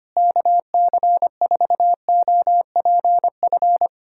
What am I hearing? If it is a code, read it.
Morse code, heard as a keyed tone, about 25 wpm, 700 Hz. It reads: KC4OPF
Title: USA Callsign DE Callsign 25wpm